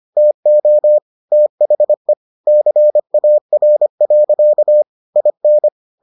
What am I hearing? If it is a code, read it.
Morse code, heard as a keyed tone, about 25 wpm, 600 Hz. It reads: TO THE CAR. IN